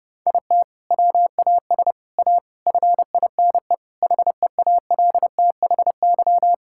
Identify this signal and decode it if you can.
Morse code, heard as a keyed tone, about 32 wpm, 700 Hz. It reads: IT WAS A FINE HEALTHY
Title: Sentences from Top 1000 Words - Review Part 5 - 30wpm